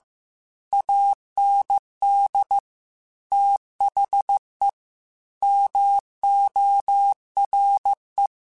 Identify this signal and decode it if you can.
Morse code, heard as a keyed tone, about 15 wpm, 800 Hz. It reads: AND THE MORE
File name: Technical Subjects / Technical Subjects-mp3s / XL @13